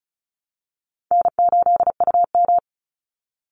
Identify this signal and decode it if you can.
Morse code, heard as a keyed tone, about 34 wpm, 700 Hz. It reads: N8UM
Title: Easy Call Signs Repeated 3x - 35wpm